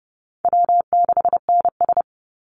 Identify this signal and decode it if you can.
Morse code, heard as a keyed tone, about 30 wpm, 700 Hz. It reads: W6NS
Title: Easy Call Signs - 30wpm